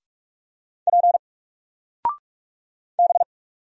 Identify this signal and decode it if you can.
Morse code, heard as a keyed tone, about 47 wpm, 700 Hz. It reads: P B
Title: Single Character (Letter:Number:Punctuation:Prosign) Alt 45wpm